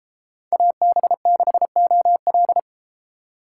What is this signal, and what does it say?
Morse code, heard as a keyed tone, about 31 wpm, 700 Hz. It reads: AB6OL